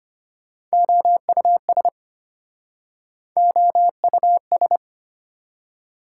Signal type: Morse code

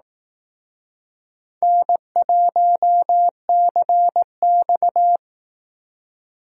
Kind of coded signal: Morse code

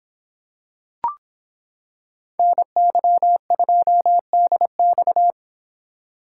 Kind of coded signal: Morse code

